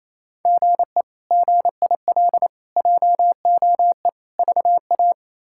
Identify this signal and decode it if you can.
Morse code, heard as a keyed tone, about 28 wpm, 700 Hz. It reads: GE GIL JOE VA
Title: K1USN SST QSO Practice - 28wpm